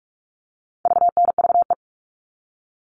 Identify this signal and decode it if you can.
Morse code, heard as a keyed tone, about 44 wpm, 700 Hz. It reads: VNVE
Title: All Cut Numbers - 4-Digit 45wpm